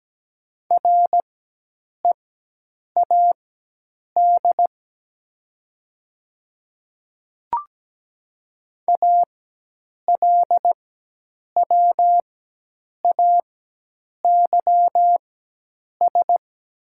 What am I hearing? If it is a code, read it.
Morse code, heard as a keyed tone, about 17 wpm, spaced as for 8 wpm, 700 Hz. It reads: READ ALWAYS
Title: 4x Character Spacing - Top 300 Words - 17wpm